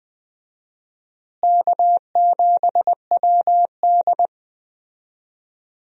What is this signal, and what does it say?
Morse code, heard as a keyed tone, about 20 wpm, 700 Hz. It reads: K7WD
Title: USA Call Signs - 4 Characters - Repeated 3x - 20wpm